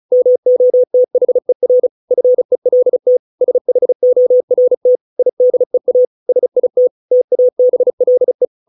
Morse code, heard as a keyed tone, about 34 wpm, 500 Hz. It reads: MOTHER FELT SHORT IDEA SIT TABLE